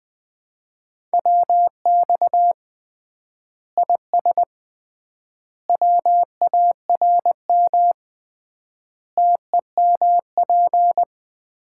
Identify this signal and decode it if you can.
Morse code, heard as a keyed tone, about 21 wpm, 700 Hz. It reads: WX IS WARM TEMP
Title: Common QSO Phrases - 3x Word-Spacing - 20wpm